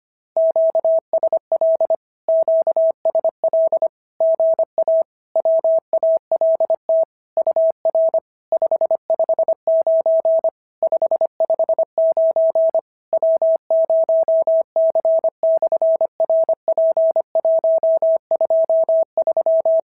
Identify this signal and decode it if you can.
Morse code, heard as a keyed tone, about 25 wpm, 650 Hz. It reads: QSL QSL GA WALT UR 559 559 W0C/RP123
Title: SOTA QSO Practice 25wpm